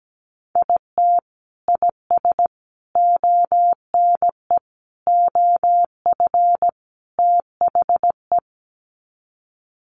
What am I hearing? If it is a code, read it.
Morse code, heard as a keyed tone, about 17 wpm, 700 Hz. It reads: IT IS ONE OF THE
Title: Sets of 5 Words 17wpm